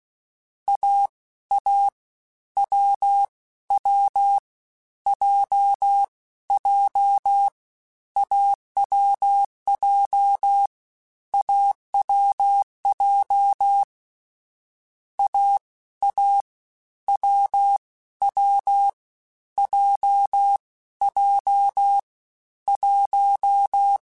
Morse code, heard as a keyed tone, about 16 wpm, 800 Hz. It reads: A A W W J J AWJ AWJ A A W W J J 1
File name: Difficult Combos / Difficult Combos-mp3s / awj1 ndb6 gz7 @15